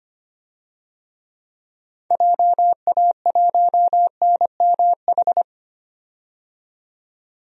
Morse code, heard as a keyed tone, about 25 wpm, 700 Hz. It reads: JA1NMH
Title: DXCC-100 Call Signs - 6 Characters - 25wpm